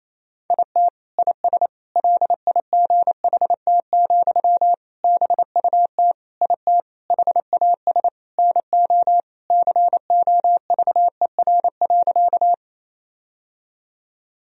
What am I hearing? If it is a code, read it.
Morse code, heard as a keyed tone, about 28 wpm, 700 Hz. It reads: IT IS LIGHT, BUT IT HAS NO COVER.
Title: Sentences from Top 1000 Words - Review Part 1 - 28wpm